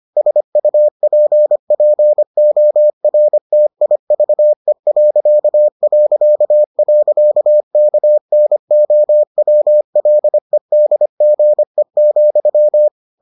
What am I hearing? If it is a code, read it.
Morse code, heard as a keyed tone, about 25 wpm, 600 Hz. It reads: SUPPORTIVE...KNOWLEDGE,